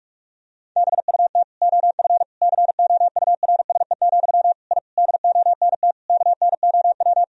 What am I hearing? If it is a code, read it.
Morse code, heard as a keyed tone, about 46 wpm, 700 Hz. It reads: BUT OF COURSE, I DONT KNOW
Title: Sentences from Top 1000 Words - Review Part 2 - 45wpm